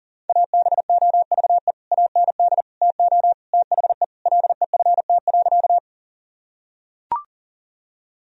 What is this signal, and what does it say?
Morse code, heard as a keyed tone, about 40 wpm, 700 Hz. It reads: ABOVE AND TO THE LEFT.
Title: Sentences from Top 1000 Words - Review Part 2 - 40wpm